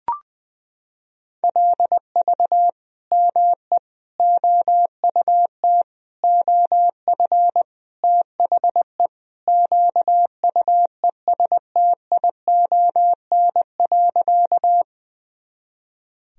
Morse code, heard as a keyed tone, about 20 wpm, 700 Hz. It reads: LV ME OUT OF THE QUESTION.
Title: Sentences from Top 400 Words and Abbreviations 20wpm